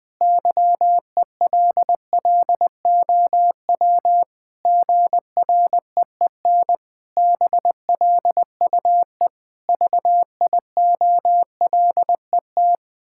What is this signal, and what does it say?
Morse code, heard as a keyed tone, about 20 wpm, 700 Hz. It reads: YELLOW GREEN BLUE VIOLET
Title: Pangram Challenge v2 - 20wpm